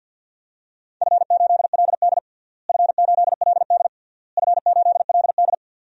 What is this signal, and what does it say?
Morse code, heard as a keyed tone, about 52 wpm, 700 Hz. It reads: F8LD F8LD F8LD
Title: DXCC-100 Call Signs - 4 Characters - Repeated 3x - 50wpm